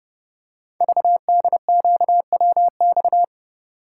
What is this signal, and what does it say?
Morse code, heard as a keyed tone, about 28 wpm, 700 Hz. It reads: VDQWX
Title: Code Groups - 5 Letter:Numbers - 30wpm